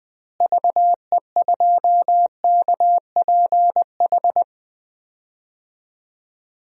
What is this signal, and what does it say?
Morse code, heard as a keyed tone, about 21 wpm, 700 Hz. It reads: VE2KPH
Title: DXCC-100 Call Signs - 6 Characters - Repeated 2x - 20wpm